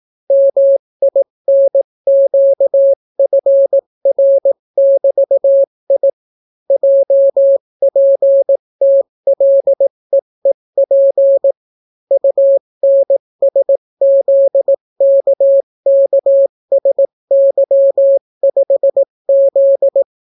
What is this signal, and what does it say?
Morse code, heard as a keyed tone, about 18 wpm, 550 Hz. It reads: MINQFR<BT>I JPTLEEP UNSZKKSY5Z